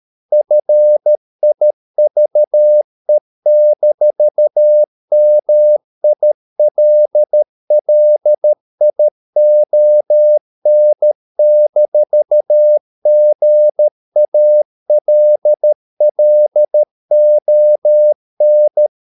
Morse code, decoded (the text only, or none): FIVE-MILLION-GALLON